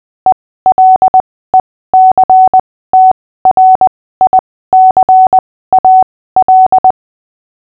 Morse code, heard as a keyed tone, about 20 wpm, spaced as for 15 wpm, 750 Hz. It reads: ELECTRICAL
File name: cw / fists / Short-Story-6-15wpm